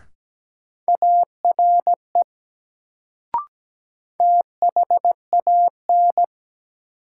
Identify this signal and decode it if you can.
Morse code, heard as a keyed tone, about 17 wpm, 700 Hz. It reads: ARE THAN
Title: Sentences from Top 100 Words - New Word Drill - 17wpm